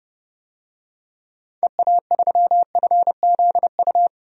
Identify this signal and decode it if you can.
Morse code, heard as a keyed tone, about 30 wpm, 700 Hz. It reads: EA3FZU
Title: DXCC-100 Call Signs - 6 Characters - 30wpm